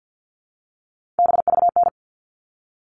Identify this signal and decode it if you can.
Morse code, heard as a keyed tone, about 50 wpm, 700 Hz. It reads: BVN
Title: All Cut Numbers - 3-Digit 50wpm